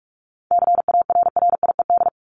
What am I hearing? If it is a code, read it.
Morse code, heard as a keyed tone, about 44 wpm, 700 Hz. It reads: CARRIED
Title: ED Suffix - Top 100 Words with Base Words 45wpm